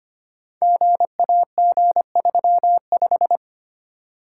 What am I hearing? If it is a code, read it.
Morse code, heard as a keyed tone, about 25 wpm, 700 Hz. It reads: GAG35
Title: Code Groups - 5 Letter:Numbers - 25wpm